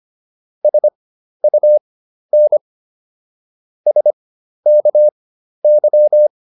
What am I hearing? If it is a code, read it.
Morse code, heard as a keyed tone, about 25 wpm, spaced as for 12 wpm, 600 Hz. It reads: SUN SKY